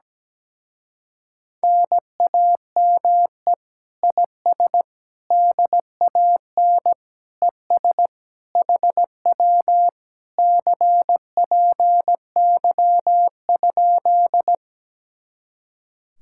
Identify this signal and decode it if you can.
Morse code, heard as a keyed tone, about 17 wpm, 700 Hz. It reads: NAME IS DAN ES HW CPY?
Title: Speed-Racing Common QSO Phrases Encore - 17wpm